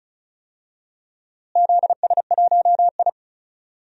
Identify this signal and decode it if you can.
Morse code, heard as a keyed tone, about 36 wpm, 700 Hz. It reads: ZS1I
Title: DXCC-100 Call Signs - 4 Characters - Repeated 2x - 35wpm